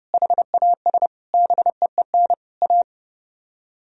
Morse code, heard as a keyed tone, about 29 wpm, 700 Hz. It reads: HAS BEEN A